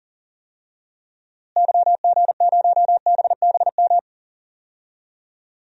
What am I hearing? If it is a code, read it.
Morse code, heard as a keyed tone, about 40 wpm, 700 Hz. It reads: YG0BBM
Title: DXCC-100 Call Signs - 6 Characters - Repeated 3x - 40wpm